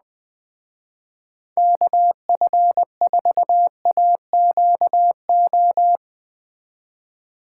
Morse code, heard as a keyed tone, about 20 wpm, 700 Hz. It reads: KF4AQO
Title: Speed Racing - USA 2x3 Call Signs 17wpm